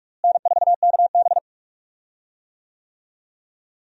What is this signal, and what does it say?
Morse code, heard as a keyed tone, about 46 wpm, 700 Hz. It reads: N4KB
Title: USA Call Signs - 4 Characters - Repeated 2x - 45wpm